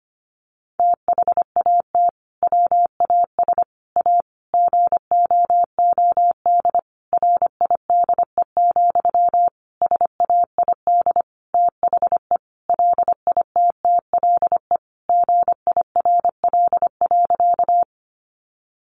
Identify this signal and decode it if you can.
Morse code, heard as a keyed tone, about 25 wpm, 700 Hz. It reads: THAT WAS A GOOD RIDE, SAID THE LITTLE GIRL.
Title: Sentences from Top 1200 Words 25wpm